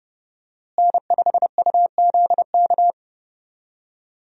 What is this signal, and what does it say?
Morse code, heard as a keyed tone, about 30 wpm, 700 Hz. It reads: N5UZK
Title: USA Call Signs - 5 Characters - Repeated 2x - 30wpm